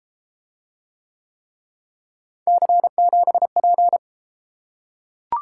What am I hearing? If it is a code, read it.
Morse code, heard as a keyed tone, about 29 wpm, 700 Hz. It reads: C7P